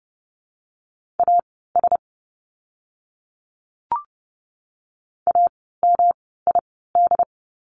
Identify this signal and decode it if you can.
Morse code, heard as a keyed tone, about 30 wpm, spaced as for 17 wpm, 700 Hz. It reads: AS AMID
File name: course / 30:10wpm / 108 Lesson 18f - Letter m - Words 30wpm